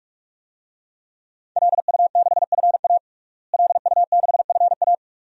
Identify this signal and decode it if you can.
Morse code, heard as a keyed tone, about 46 wpm, 700 Hz. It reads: LU6FA LU6FA
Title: DXCC-100 Call Signs - 5 Characters - Repeated 2x - 45wpm